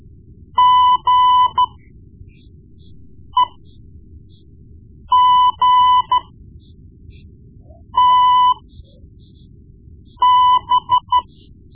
Morse code, heard as a keyed tone, about 13 wpm, spaced as for 4 wpm, 1000 Hz. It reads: GEGTB
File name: Morse Lessons / Morse Lessons / Tape-1B